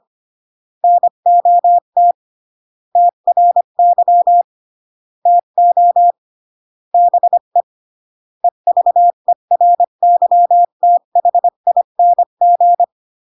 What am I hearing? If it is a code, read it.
Morse code, heard as a keyed tone, about 25 wpm, 700 Hz. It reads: NOT TRY TO BE EVERYTHING